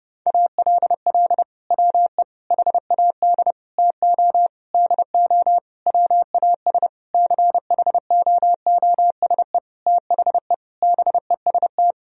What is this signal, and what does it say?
Morse code, heard as a keyed tone, about 30 wpm, 700 Hz. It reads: ALL WE HAD TO DO WAS CHOOSE THE BEST
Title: Sentences from Top 1000 Words - Review Part 4 - 30wpm